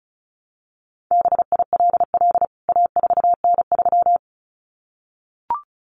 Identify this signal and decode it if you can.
Morse code, heard as a keyed tone, about 34 wpm, 700 Hz. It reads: BILL A4N3
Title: CWT Exchange 35wpm